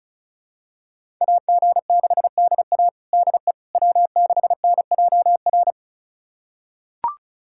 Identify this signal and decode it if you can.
Morse code, heard as a keyed tone, about 34 wpm, 700 Hz. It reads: AG6DA DE W6NJR
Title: USA Callsign DE Callsign - 5 Characters - 35wpm